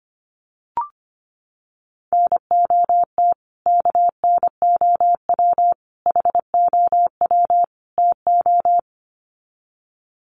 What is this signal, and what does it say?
Morse code, heard as a keyed tone, about 25 wpm, 700 Hz. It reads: NOT KNOW HOW TO
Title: Sets of 4 Words 25wpm